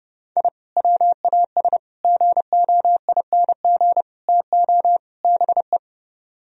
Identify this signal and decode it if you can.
Morse code, heard as a keyed tone, about 29 wpm, 700 Hz. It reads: I WAS GOING TO BE